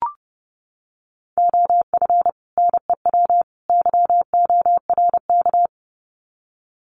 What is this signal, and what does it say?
Morse code, heard as a keyed tone, about 30 wpm, 700 Hz. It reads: OF NEW YORK
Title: Sets of 3 Words 30wpm